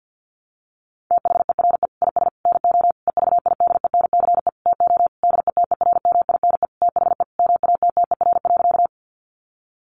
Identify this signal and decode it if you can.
Morse code, heard as a keyed tone, about 50 wpm, 700 Hz. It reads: THERE IS NO EVIDENCE TO DETERMINE THE MATTER.
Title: Sentences from Top 1300 Words 50wpm